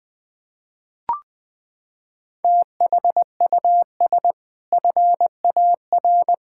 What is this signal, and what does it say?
Morse code, heard as a keyed tone, about 20 wpm, 700 Hz. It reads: THUS FAR
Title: Sentences from Top 1000 Words - Review Part 5 - 20wpm